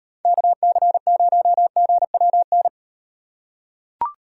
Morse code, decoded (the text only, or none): KC0GWN